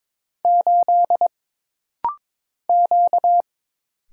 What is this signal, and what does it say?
Morse code, heard as a keyed tone, about 22 wpm, 700 Hz. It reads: 8 Q
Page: This rapid-fire timing is geared to push your ICR (Instant Character Recognition) skills.